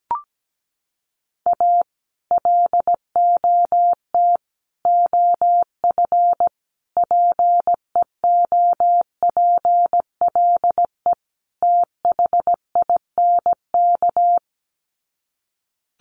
Morse code, heard as a keyed tone, about 17 wpm, 700 Hz. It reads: A LOT OF PEOPLE THINK
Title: Sets of 5 Words Encore 17wpm